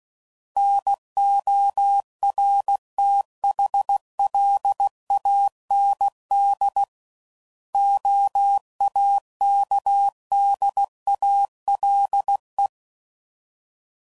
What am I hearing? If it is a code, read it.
Morse code, heard as a keyed tone, about 16 wpm, 800 Hz. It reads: NORTHLAND OAKDALE
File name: CW Code Groups @15WPM / fromSM's / Cities